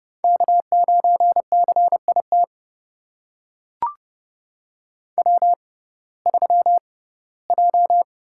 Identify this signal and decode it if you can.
Morse code, heard as a keyed tone, about 31 wpm, spaced as for 19 wpm, 700 Hz. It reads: K9CIT W3J
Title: USA Call Signs - 5 Characters - 6x Character Spacing - 30wpm